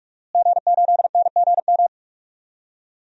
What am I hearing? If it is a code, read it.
Morse code, heard as a keyed tone, about 47 wpm, 700 Hz. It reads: G8NGM
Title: DXCC-100 Call Signs - 5 Characters - Repeated 3x - 45wpm